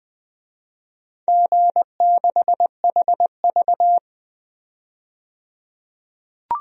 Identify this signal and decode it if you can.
Morse code, heard as a keyed tone, about 20 wpm, 700 Hz. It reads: G6HV
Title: DXCC-100 Call Signs - 4 Characters - 20wpm